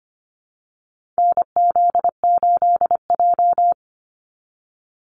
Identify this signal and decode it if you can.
Morse code, heard as a keyed tone, about 25 wpm, 700 Hz. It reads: NZ8J
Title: Call Signs - 25wpm